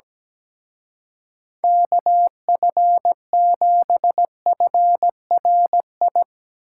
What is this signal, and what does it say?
Morse code, heard as a keyed tone, about 17 wpm, 700 Hz. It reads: KF7FRI
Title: Speed Racing - USA 2x3 Call Signs 17wpm